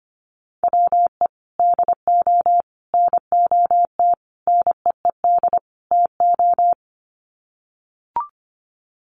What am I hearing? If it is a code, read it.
Morse code, heard as a keyed tone, about 25 wpm, 700 Hz. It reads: WE DO NOT NEED TO
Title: Sets of 5 Words Encore 25wpm